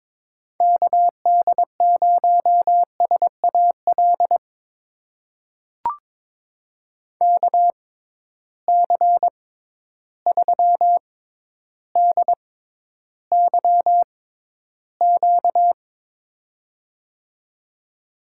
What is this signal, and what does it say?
Morse code, heard as a keyed tone, about 22 wpm, spaced as for 11 wpm, 700 Hz. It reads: KD0SAL KC3DYQ
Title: USA Call Signs - 6 Characters - 6x Character Spacing - 22wpm